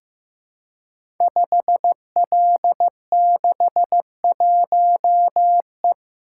Morse code, heard as a keyed tone, about 15 wpm, 700 Hz. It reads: 5L61E
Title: Code Groups - 5 Letter:Numbers - 15wpm